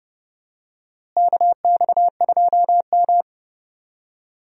Morse code, heard as a keyed tone, about 30 wpm, 700 Hz. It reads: KX2M
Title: USA Call Signs - 4 Characters - Repeated 3x - 30wpm